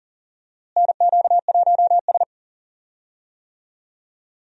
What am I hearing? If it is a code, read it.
Morse code, heard as a keyed tone, about 39 wpm, 700 Hz. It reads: NQ1S
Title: USA Call Signs - 4 Characters - Repeated 2x - 40wpm